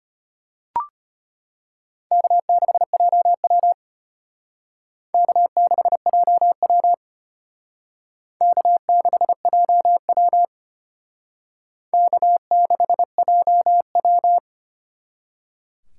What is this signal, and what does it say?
Morse code, heard as a keyed tone, about 32 wpm, 700 Hz. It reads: K6JW K6JW K6JW K6JW